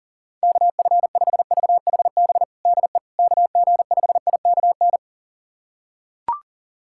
Morse code, heard as a keyed tone, about 37 wpm, 700 Hz. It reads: KF5VHB DE KG5IKN
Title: USA Callsign DE Callsign - 6 Characters - 40wpm